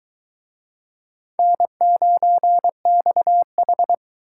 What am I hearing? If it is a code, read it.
Morse code, heard as a keyed tone, about 23 wpm, 700 Hz. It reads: N9XH